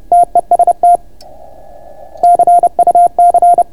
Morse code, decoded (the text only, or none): TEST CUC